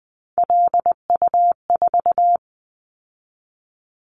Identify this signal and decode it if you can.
Morse code, heard as a keyed tone, about 19 wpm, 700 Hz. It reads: LU4
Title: Callsign Prefixes for Top 50 DXCC - 20wpm